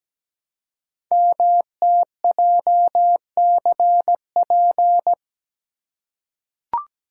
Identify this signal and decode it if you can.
Morse code, heard as a keyed tone, about 17 wpm, 700 Hz. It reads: MTJCP
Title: Code Groups - 5 Letters - 17wpm